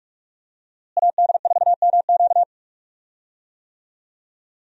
Morse code, heard as a keyed tone, about 46 wpm, 700 Hz. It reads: AD4MQ
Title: USA Call Signs - 5 Characters - Repeated 3x - 45wpm